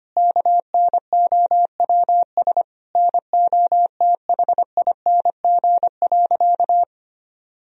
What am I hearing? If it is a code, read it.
Morse code, heard as a keyed tone, about 25 wpm, 700 Hz. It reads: KNOWS NOTHING.
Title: Sentences from Top 1000 Words - Review Part 1 - 25wpm